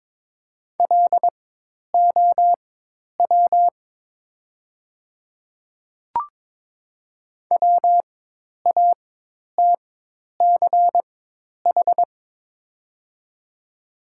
Morse code, heard as a keyed tone, about 22 wpm, spaced as for 10 wpm, 700 Hz. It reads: LOW WATCH
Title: 4x Character Spacing - Top 300 Words - 22wpm